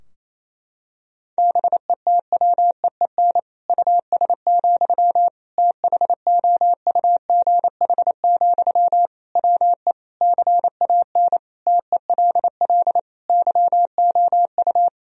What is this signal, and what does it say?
Morse code, heard as a keyed tone, about 28 wpm, 700 Hz. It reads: BETWEEN US, THOUGH, WE CAN TELL YOU
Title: Sentences from Top 1000 Words - Review Part 2 - 28wpm